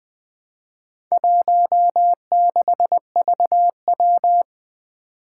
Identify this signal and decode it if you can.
Morse code, heard as a keyed tone, about 20 wpm, 700 Hz. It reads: 16VW
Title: Four Challenging Letter-Number Combinations 20wpm